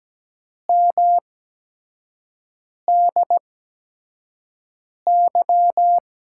Morse code, heard as a keyed tone, about 17 wpm, spaced as for 5 wpm, 700 Hz. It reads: MDY
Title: USA Call Signs - 8x Character Spacing - 17wpm